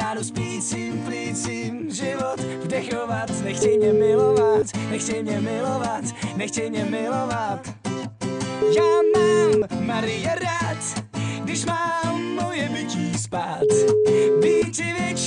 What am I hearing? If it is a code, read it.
Ringback tone (Europe, 425 Hz), with speech.